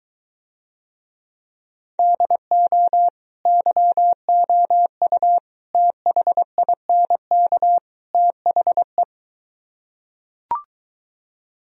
Morse code, heard as a keyed tone, about 23 wpm, 700 Hz. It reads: DO YOU THINK THE